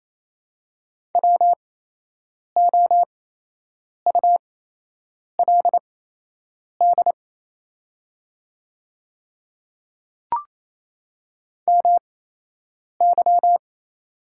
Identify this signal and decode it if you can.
Morse code, heard as a keyed tone, about 28 wpm, spaced as for 8 wpm, 690 Hz. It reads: WOULD MY
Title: 8x Character Spacing - Top 200 Words - 28wpm